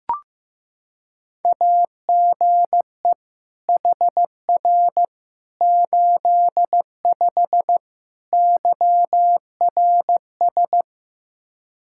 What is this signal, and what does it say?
Morse code, heard as a keyed tone, about 15 wpm, 700 Hz. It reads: AGE HR 85 YRS
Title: Practice QSO - Exchange 3 - More Info - 15wpm